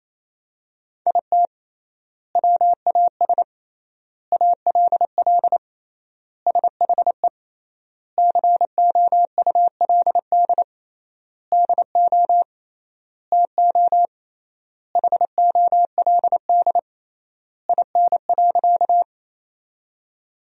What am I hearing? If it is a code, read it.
Morse code, heard as a keyed tone, about 27 wpm, 700 Hz. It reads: IT WAS ALL SHE COULD DO TO HOLD IN.
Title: 3x Word-Spacing - Sentences from Top 400 Words - 28wpm